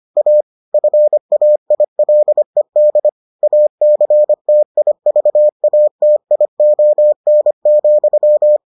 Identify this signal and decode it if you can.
Morse code, heard as a keyed tone, about 25 wpm, 600 Hz. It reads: A FAILED ACTIVATION,